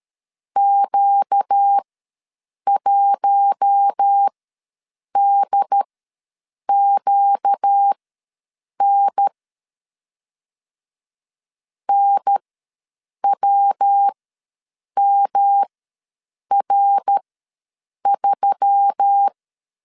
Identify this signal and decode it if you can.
Morse code, heard as a keyed tone, about 13 wpm, spaced as for 7 wpm, 800 Hz. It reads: Q1DQN NWMR3
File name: LEZIONI / 5 LBVC / 5 LB ws 12wpm 3034